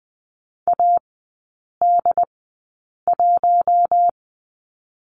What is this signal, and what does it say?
Morse code, heard as a keyed tone, about 20 wpm, spaced as for 8 wpm, 700 Hz. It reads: AD1
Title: 067 Lesson 12e - Letter d - Three Characters 20wpm